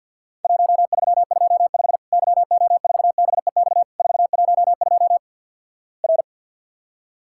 Morse code, heard as a keyed tone, about 50 wpm, 700 Hz. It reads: 1325 YO4BEX 412 R
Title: Worked All Europe DX Contest - QTC Exchange - 50wpm